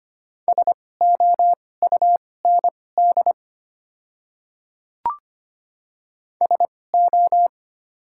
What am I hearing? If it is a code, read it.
Morse code, heard as a keyed tone, about 25 wpm, spaced as for 18 wpm, 690 Hz. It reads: SOUND SO
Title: 2x Character Spacing - Top 100 Words - 25wpm